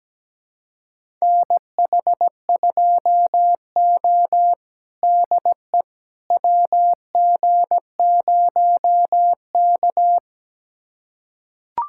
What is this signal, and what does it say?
Morse code, heard as a keyed tone, about 17 wpm, 700 Hz. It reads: NH2O DE WG0K
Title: USA Callsign DE Callsign - 4 Characters - 17wpm